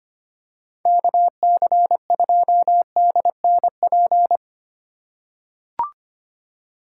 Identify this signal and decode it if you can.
Morse code, heard as a keyed tone, about 25 wpm, 700 Hz. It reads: KC2DNP
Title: USA Call Signs - 6 Characters - 2x Character Spacing - 25wpm